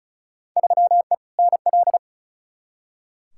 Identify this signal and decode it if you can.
Morse code, heard as a keyed tone, about 36 wpm, 700 Hz. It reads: 3E NL
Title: Field Day Exchanges Encore - 35wpm